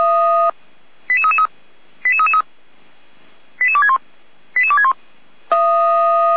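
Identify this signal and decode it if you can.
Five-tone selective call (ZVEI), heard as quick runs of five tones, tones of about 70 ms. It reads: 88333 88261 88261
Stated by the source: Five-tone ZVEI selective call received by SDR, from feuerwehr-orlamuende.org fire brigade audio archive.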